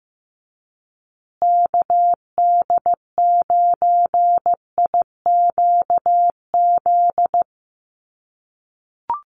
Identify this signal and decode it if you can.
Morse code, heard as a keyed tone, about 15 wpm, 700 Hz. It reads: KD9IQZ
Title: USA Call Signs Repeated 2x 15wpm